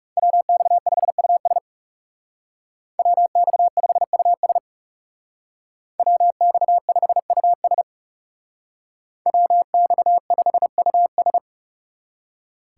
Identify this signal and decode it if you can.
Morse code, heard as a keyed tone, about 38 wpm, 700 Hz. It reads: WX5US WX5US WX5US WX5US